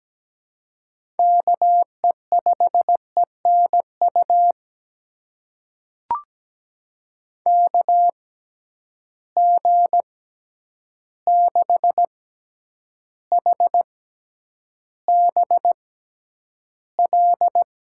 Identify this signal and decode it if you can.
Morse code, heard as a keyed tone, about 17 wpm, spaced as for 9 wpm, 700 Hz. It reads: KE5ENU KG6HBL
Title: USA Call Signs - 6 Characters - 6x Character Spacing - 17wpm